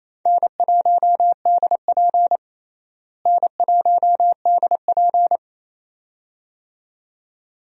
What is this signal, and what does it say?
Morse code, heard as a keyed tone, about 28 wpm, 700 Hz. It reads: N1DP N1DP
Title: USA Call Signs - 4 Characters - Repeated 2x - 28wpm